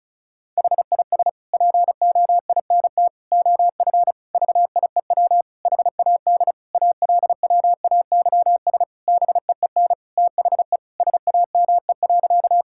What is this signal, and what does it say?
Morse code, heard as a keyed tone, about 36 wpm, 700 Hz. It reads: HIS POINT OF VIEW HAD ALWAYS BEEN THE SAME.
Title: Sentences from Top 1000 Words - Review Part 6 - 35wpm